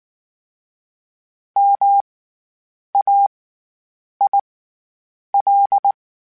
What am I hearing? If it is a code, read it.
Morse code, heard as a keyed tone, about 19 wpm, spaced as for 8 wpm, 800 Hz. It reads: MAIL
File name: Special INGs / Special INGs-mp3s / SlowToFaster ing @18